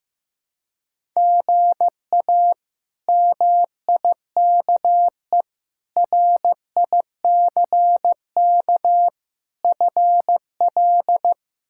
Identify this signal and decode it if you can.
Morse code, heard as a keyed tone, about 15 wpm, 700 Hz. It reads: GA MIKE RICK FL
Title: K1USN SST QSO Practice - 15wpm